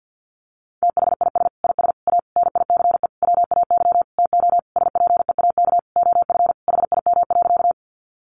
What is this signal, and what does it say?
Morse code, heard as a keyed tone, about 50 wpm, 700 Hz. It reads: THIS IS A NICE WAY TO SPEAK OF HIM.
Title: Sentences from Top 1100 Words 50wpm